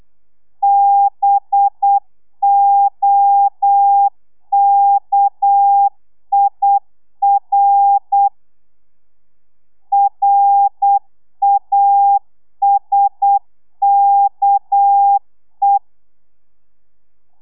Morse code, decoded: BOKIR RASKE